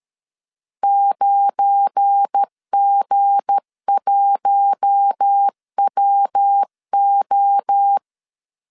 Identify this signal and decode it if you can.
Morse code, heard as a keyed tone, about 13 wpm, 800 Hz. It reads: 9G1WO
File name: LEZIONI / 7 GF9Z / 7 GF9 12wpm 2319